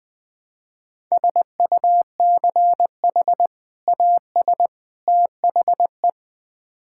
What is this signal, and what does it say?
Morse code, heard as a keyed tone, about 20 wpm, 700 Hz. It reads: SUCH AS THE
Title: Build A Phrase - 3-Word Phrases - 20wpm